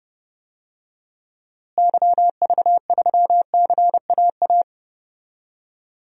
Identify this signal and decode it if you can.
Morse code, heard as a keyed tone, about 30 wpm, 700 Hz. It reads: YV3CAA
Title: DXCC-100 Call Signs - 6 Characters - Repeated 2x - 30wpm